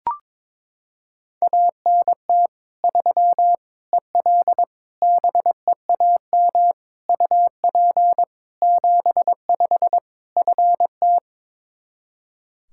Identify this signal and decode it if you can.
Morse code, heard as a keyed tone, about 22 wpm, 700 Hz. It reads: ANT 3 EL BEAM UP 75 FT
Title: Practice QSO - Exchange 2 - Info - 22wpm